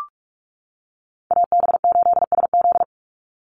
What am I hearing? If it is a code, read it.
Morse code, heard as a keyed tone, about 44 wpm, 700 Hz. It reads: AB8SZ
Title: USA Call Signs 45wpm